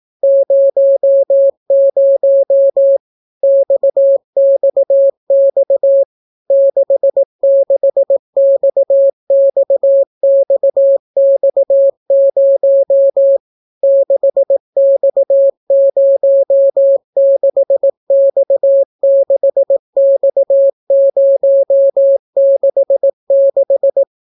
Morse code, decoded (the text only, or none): 00 XXX 66XXXX0 6X06X6X066